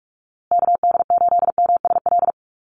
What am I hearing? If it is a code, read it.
Morse code, heard as a keyed tone, about 44 wpm, 700 Hz. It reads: KD8MSL
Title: USA Call Signs Repeated 3x 45wpm